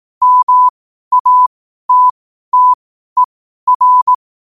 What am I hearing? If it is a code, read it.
Morse code, heard as a keyed tone, about 18 wpm, spaced as for 12 wpm, 1000 Hz. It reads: MATTER
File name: qotdfiles / 12wpm